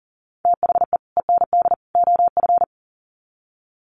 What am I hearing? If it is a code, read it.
Morse code, heard as a keyed tone, about 39 wpm, 700 Hz. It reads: THE END OF
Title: Sets of 3 Words 40wpm